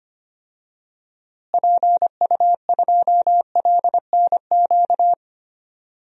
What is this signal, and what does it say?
Morse code, heard as a keyed tone, about 25 wpm, 700 Hz. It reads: PU2LNQ
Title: DXCC-100 Call Signs - 6 Characters - Repeated 2x - 25wpm